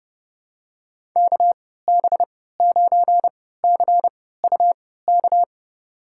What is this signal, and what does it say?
Morse code, heard as a keyed tone, about 30 wpm, spaced as for 17 wpm, 700 Hz. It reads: KB9CUK
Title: USA Call Signs - 6 Characters - 3x Character Spacing - 30wpm